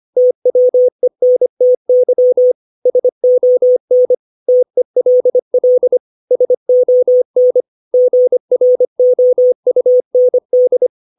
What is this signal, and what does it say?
Morse code, heard as a keyed tone, about 25 wpm, 500 Hz. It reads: TWENTY SON TELL SON GROUND
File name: 632 words / 632 Words 25WPM